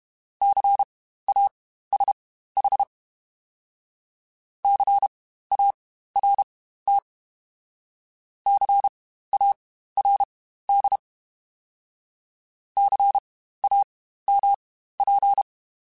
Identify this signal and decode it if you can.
Morse code, heard as a keyed tone, about 32 wpm, spaced as for 15 wpm, 800 Hz. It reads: CASH CART CARD CAMP
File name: Spaced Alphas / Spaced Alphas mp3s / Cs sp @30